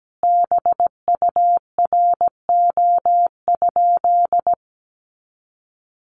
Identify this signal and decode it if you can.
Morse code, heard as a keyed tone, about 17 wpm, 700 Hz. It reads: BURO?
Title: Common QSO Phrases 17wpm